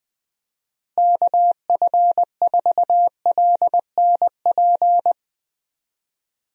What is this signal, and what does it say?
Morse code, heard as a keyed tone, about 20 wpm, 700 Hz. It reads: KF4LNP
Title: USA Call Signs - 6 Characters - Repeated 3x - 20wpm